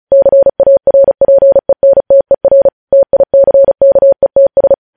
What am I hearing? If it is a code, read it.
Morse code, heard as a keyed tone, about 36 wpm, 550 Hz. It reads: CARPENTER TICKETS
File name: podcast / News headlines-1x-35x35